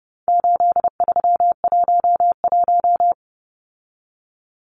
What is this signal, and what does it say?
Morse code, heard as a keyed tone, about 29 wpm, 700 Hz. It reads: 8311
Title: Numbers - 4-Digit 30wpm